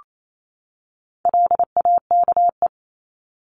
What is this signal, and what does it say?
Morse code, heard as a keyed tone, about 28 wpm, 700 Hz. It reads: LAKE
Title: Top 1001-1250 Words 28wpm